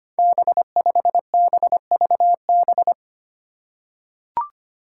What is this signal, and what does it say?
Morse code, heard as a keyed tone, about 25 wpm, 700 Hz. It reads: B5BVB
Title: Five Challenging Letter-Number Combinations 25wpm